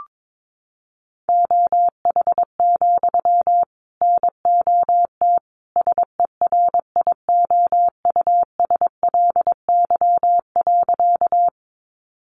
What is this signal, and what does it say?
Morse code, heard as a keyed tone, about 22 wpm, 700 Hz. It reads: OH, NOT SERIOUSLY.
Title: Sentences from Top 1200 Words 22wpm